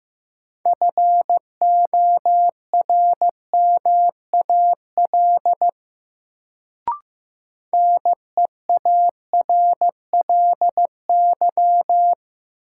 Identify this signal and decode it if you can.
Morse code, heard as a keyed tone, about 15 wpm, 700 Hz. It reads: FORMAL NEARLY
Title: Sentences from Top 1300 Words - New Word Drill - 15wpm